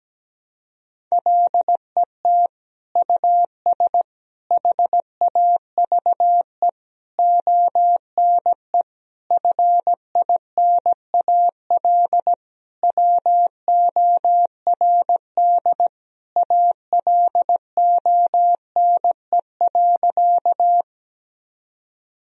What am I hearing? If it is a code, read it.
Morse code, heard as a keyed tone, about 17 wpm, 700 Hz. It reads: LET US HAVE ONE FINAL WORD ALONE.
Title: Sentences from Top 1000 Words - Review Part 2 - 17wpm